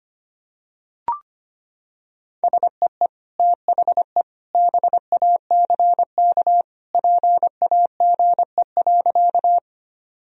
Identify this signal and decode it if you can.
Morse code, heard as a keyed tone, about 25 wpm, 700 Hz. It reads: SEE THE BACK PAGE.
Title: Sentences from Top 1000 Words - Review Part 1 - 25wpm